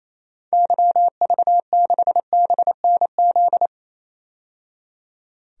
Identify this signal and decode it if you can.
Morse code, heard as a keyed tone, about 28 wpm, 700 Hz. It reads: YV6BNZ
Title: DXCC-100 Call Signs - 6 Characters - 28wpm